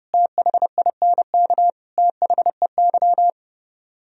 Morse code, heard as a keyed tone, about 34 wpm, 700 Hz. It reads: THINK THEY